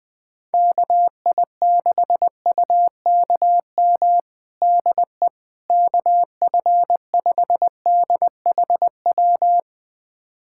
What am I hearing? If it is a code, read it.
Morse code, heard as a keyed tone, about 20 wpm, 700 Hz. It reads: KI6UKM DE KF5DHW
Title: USA Callsign DE Callsign - 6 Characters - 20wpm